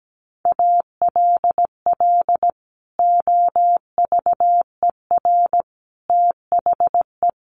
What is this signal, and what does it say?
Morse code, heard as a keyed tone, about 17 wpm, 700 Hz. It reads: ALL OVER THE
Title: Sets of 4 Words 17wpm